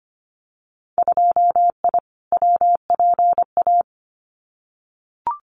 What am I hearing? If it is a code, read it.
Morse code, heard as a keyed tone, about 25 wpm, 700 Hz. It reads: 2I WPA
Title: Winter Field Day Exchanges 25wpm